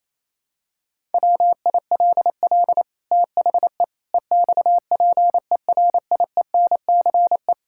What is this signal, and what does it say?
Morse code, heard as a keyed tone, about 27 wpm, 700 Hz. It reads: WILL THE EXPERIENCE
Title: Sentences from Top 1000 Words - Review Part 5 - 28wpm